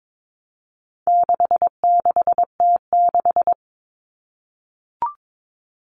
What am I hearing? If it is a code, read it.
Morse code, heard as a keyed tone, about 22 wpm, 700 Hz. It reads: 66T6
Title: Common Cut Numbers - 4-Digit 22wpm